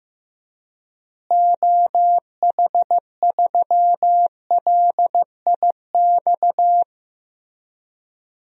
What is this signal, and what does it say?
Morse code, heard as a keyed tone, about 15 wpm, 700 Hz. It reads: OH3LIX
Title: DXCC-100 Call Signs - 6 Characters - Repeated 2x - 15wpm